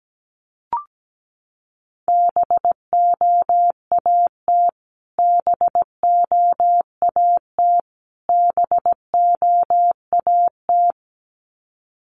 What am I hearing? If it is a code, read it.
Morse code, heard as a keyed tone, about 17 wpm, 700 Hz. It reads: BOAT BOAT BOAT
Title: Top 500 Words Repeated 17wpm